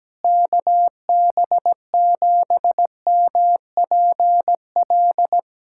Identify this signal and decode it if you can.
Morse code, heard as a keyed tone, about 17 wpm, 700 Hz. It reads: KB7MPL
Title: USA Callsign DE Callsign - 6 Characters - 17wpm